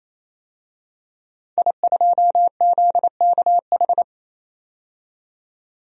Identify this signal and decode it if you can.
Morse code, heard as a keyed tone, about 28 wpm, 700 Hz. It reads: I2ZKH
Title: DXCC-100 Call Signs - 5 Characters - 28wpm